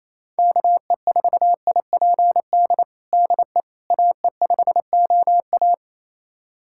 Morse code, heard as a keyed tone, about 28 wpm, 700 Hz. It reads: KE4IPD DE AE5OA
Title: USA Callsign DE Callsign 28wpm